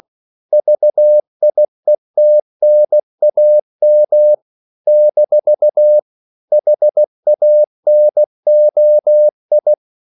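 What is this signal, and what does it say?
Morse code, heard as a keyed tone, about 16 wpm, 600 Hz. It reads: VIETNAM - HANOI
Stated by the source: CW practice of national capitals at 16 words per minute, from ON6ZQ.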